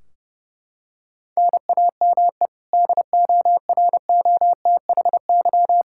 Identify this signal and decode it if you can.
Morse code, heard as a keyed tone, about 31 wpm, 700 Hz. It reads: NAME DOROTHY
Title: Practice QSO - Exchange 1 - Beginning - 30wpm